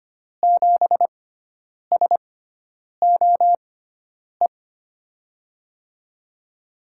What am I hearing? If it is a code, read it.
Morse code, heard as a keyed tone, about 25 wpm, spaced as for 9 wpm, 700 Hz. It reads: 7SOE
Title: USA Call Signs - 6 Characters - 6x Character Spacing - 25wpm